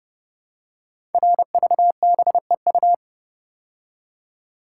Morse code, heard as a keyed tone, about 30 wpm, 700 Hz. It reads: RVBEU
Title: Code Groups - 5 Letters - 30wpm